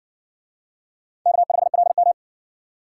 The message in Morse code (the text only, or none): K5LM